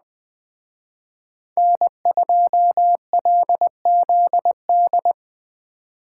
Morse code, heard as a keyed tone, about 20 wpm, 700 Hz. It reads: N2LZD
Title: Speed Racing - USA 1x3 Call Signs 17wpm